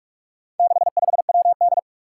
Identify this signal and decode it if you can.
Morse code, heard as a keyed tone, about 46 wpm, 700 Hz. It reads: 65WD
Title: Four Challenging Letter-Number Combinations 45wpm